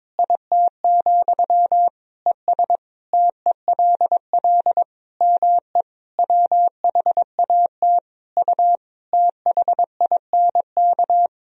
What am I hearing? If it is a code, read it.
Morse code, heard as a keyed tone, about 22 wpm, 700 Hz. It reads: IT, ES TELL ME WHAT U THINK
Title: Sentences from Top 200 Words and Abbreviations 22wpm